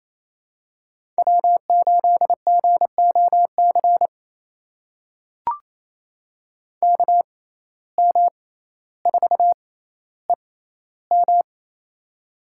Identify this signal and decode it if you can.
Morse code, heard as a keyed tone, about 28 wpm, spaced as for 14 wpm, 700 Hz. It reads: W8GOC KM4EM
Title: USA Call Signs - 5 Characters - 6x Character Spacing - 28wpm